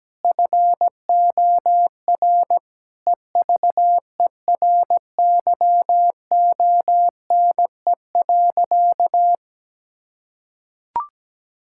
Morse code, heard as a keyed tone, about 17 wpm, 700 Hz. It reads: FOR EVERYONE.
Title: Sentences from Top 1000 Words - Review Part 5 - 17wpm